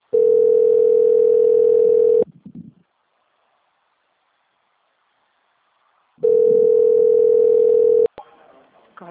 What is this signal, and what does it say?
Ringback tone (North America), 440 + 480 Hz.